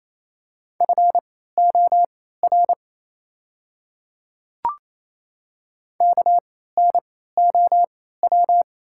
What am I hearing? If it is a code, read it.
Morse code, heard as a keyed tone, about 28 wpm, spaced as for 16 wpm, 690 Hz. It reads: FOR KNOW
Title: 3x Character Spacing - Top 100 Words - 28wpm